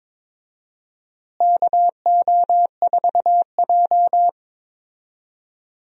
Morse code, heard as a keyed tone, about 22 wpm, 700 Hz. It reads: KO4J
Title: USA Call Signs - 4 Characters - Repeated 2x - 22wpm